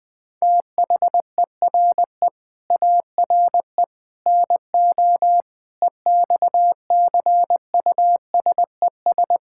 Morse code, heard as a keyed tone, about 21 wpm, 700 Hz. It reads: THERE ARE NO EXCUSES
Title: Sentences with 3-Word Phrases - 20wpm